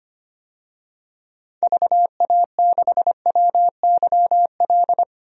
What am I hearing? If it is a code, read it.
Morse code, heard as a keyed tone, about 25 wpm, 700 Hz. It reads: VA6WYL
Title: DXCC-100 Call Signs - 6 Characters - 25wpm